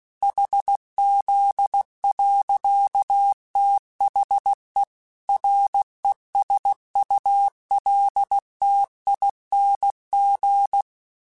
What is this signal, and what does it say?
Morse code, heard as a keyed tone, about 16 wpm, 800 Hz. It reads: HZ. THE RESULTING